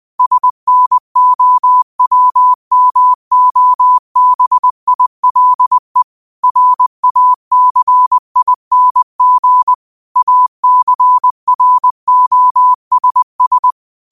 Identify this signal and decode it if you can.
Morse code, heard as a keyed tone, about 20 wpm, 1000 Hz. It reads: SNOWMOBILE RACING ACROSS